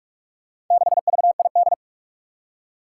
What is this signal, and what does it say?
Morse code, heard as a keyed tone, about 46 wpm, 700 Hz. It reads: 6VID
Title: Four Challenging Letter-Number Combinations 45wpm